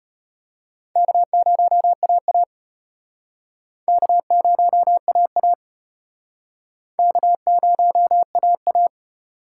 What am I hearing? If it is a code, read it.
Morse code, heard as a keyed tone, about 34 wpm, 700 Hz. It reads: K0AA K0AA K0AA